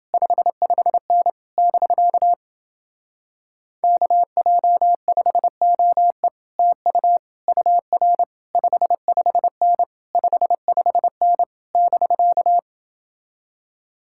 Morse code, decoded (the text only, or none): 55N <BK> KJ5OE TU UR 55N 55N <BK>